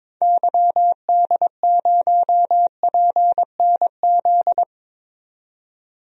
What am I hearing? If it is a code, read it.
Morse code, heard as a keyed tone, about 21 wpm, 700 Hz. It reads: YD0PNZ
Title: DXCC-100 Call Signs - 6 Characters - 22wpm